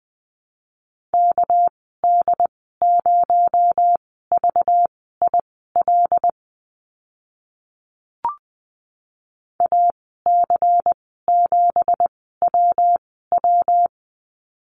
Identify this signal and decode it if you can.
Morse code, heard as a keyed tone, about 20 wpm, spaced as for 15 wpm, 700 Hz. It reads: KD0VIL AC7WW
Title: 219 Lesson 35g - Number 0 - Call Signs 20wpm